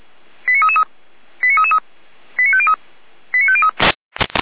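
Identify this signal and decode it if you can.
Five-tone selective call (ZVEI), heard as quick runs of five tones, tones of about 70 ms. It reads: 89333 89333 89503 89503
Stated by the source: ZVEI five-tone selective call received on an SDR.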